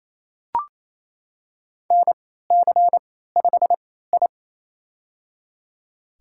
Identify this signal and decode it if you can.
Morse code, heard as a keyed tone, about 28 wpm, spaced as for 16 wpm, 700 Hz. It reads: NC5I
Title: USA Call Signs - 4 Characters - 3x Character Spacing - 28wpm